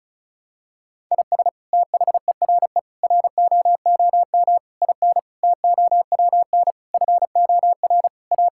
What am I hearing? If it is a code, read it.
Morse code, heard as a keyed tone, about 36 wpm, 700 Hz. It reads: IS THERE ROOM IN TOWN FOR A
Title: Sentences from Top 1000 Words - Review Part 5 - 35wpm